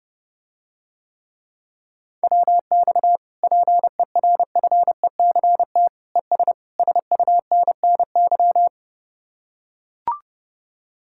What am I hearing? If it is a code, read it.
Morse code, heard as a keyed tone, about 30 wpm, 700 Hz. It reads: WX PERFECT ES SUNNY